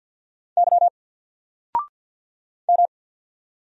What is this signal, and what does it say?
Morse code, heard as a keyed tone, about 48 wpm, 700 Hz. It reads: Y M
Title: Single Character (Letter:Number:Punctuation:Prosign) Alt 50wpm